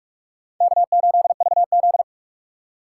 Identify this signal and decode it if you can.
Morse code, heard as a keyed tone, about 44 wpm, 700 Hz. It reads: K8VZ